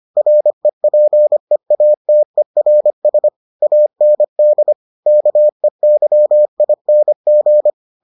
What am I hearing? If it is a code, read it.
Morse code, heard as a keyed tone, about 25 wpm, 600 Hz. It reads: REPEATERS AND KEYING